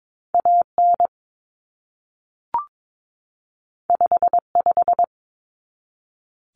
Morse code, heard as a keyed tone, about 22 wpm, 700 Hz. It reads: AN 55
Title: Common Cut Numbers - 2-Digit 22wpm